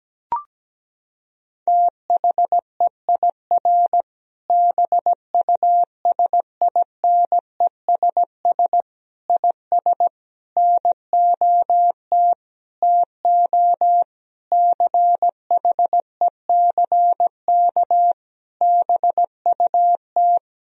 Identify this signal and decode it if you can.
Morse code, heard as a keyed tone, about 17 wpm, 700 Hz. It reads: THEIR BUSINESS IS NOT TO CHECK BUT
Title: Sentences from Top 1000 Words - Review Part 3 - 17wpm